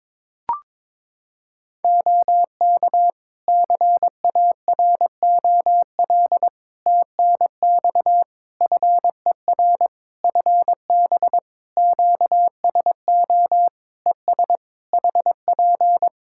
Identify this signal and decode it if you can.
Morse code, heard as a keyed tone, about 22 wpm, 700 Hz. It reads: OK CAROL TNX FER FB QSO ES HP
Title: Practice QSO - Exchange 4 - Ending - 22wpm